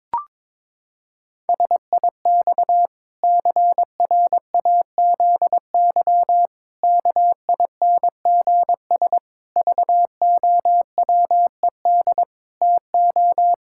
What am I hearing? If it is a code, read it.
Morse code, heard as a keyed tone, about 22 wpm, 700 Hz. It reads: SIX CRAZY KINGS VOWED TO
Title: Pangram Challenge v2 - 22wpm